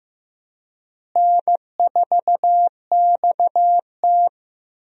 Morse code, heard as a keyed tone, about 15 wpm, 700 Hz. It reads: N4XT
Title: USA Call Signs - 4 Characters - Repeated 2x - 15wpm